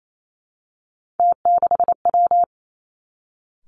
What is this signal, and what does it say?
Morse code, heard as a keyed tone, about 28 wpm, 700 Hz. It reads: T6W
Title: Three Challenging Letter-Number Combinations 28wpm